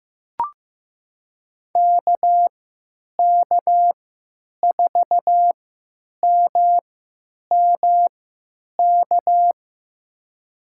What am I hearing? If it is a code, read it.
Morse code, heard as a keyed tone, about 15 wpm, spaced as for 9 wpm, 700 Hz. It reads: KK4MMK
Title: USA Call Signs - 6 Characters - 3x Character Spacing - 15wpm